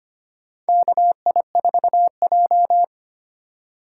Morse code, heard as a keyed tone, about 25 wpm, 700 Hz. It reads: KI4J
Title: USA Call Signs - 4 Characters - 4x Character Spacing - 25wpm